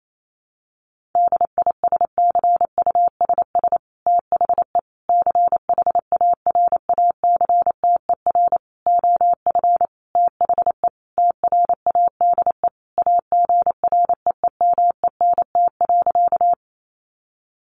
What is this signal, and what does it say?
Morse code, heard as a keyed tone, about 28 wpm, 700 Hz. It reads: DISCUSS THE CHARACTER OF THE TRADE AGREEMENT.
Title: Sentences from Top 1500 Words 28wpm